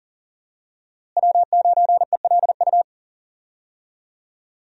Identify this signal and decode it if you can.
Morse code, heard as a keyed tone, about 40 wpm, 700 Hz. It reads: W9ELU
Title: USA Call Signs - 5 Characters - Repeated 2x - 40wpm